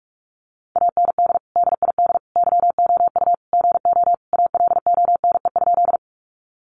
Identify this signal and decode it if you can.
Morse code, heard as a keyed tone, about 44 wpm, 700 Hz. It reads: AND DID YOU GO ALONE?
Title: Sentences from Top 1000 Words 45wpm